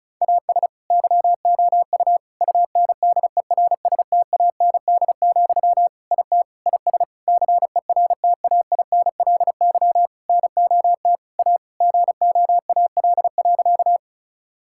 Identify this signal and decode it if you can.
Morse code, heard as a keyed tone, about 34 wpm, 700 Hz. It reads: AS YOU UNDERSTAND, IT IS CERTAINLY NOT A GOAL.
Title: Sentences from Top 1000 Words - Review Part 5 - 35wpm